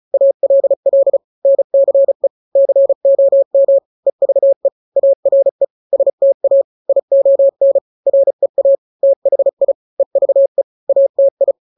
Morse code, heard as a keyed tone, about 35 wpm, 550 Hz. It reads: ALL NCE COM EVE ARE STA ION REA THI EVE ATI
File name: 3-grams 35 30 WPM